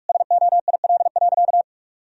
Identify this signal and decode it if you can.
Morse code, heard as a keyed tone, about 41 wpm, 700 Hz. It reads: SOIL.
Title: Sentences from Top 1000 Words - Review Part 4 - 45wpm